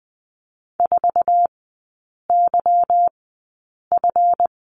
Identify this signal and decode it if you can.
Morse code, heard as a keyed tone, about 20 wpm, spaced as for 8 wpm, 700 Hz. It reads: 4YF
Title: 152 Lesson 25e - Letter p - Three Characters 20wpm